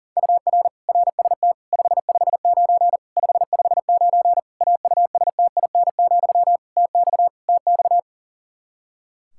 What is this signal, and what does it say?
Morse code, heard as a keyed tone, about 40 wpm, 700 Hz. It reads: UR RST 559 559 AUSTIN, TX TX
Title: K1USN SST QSO Practice 40wpm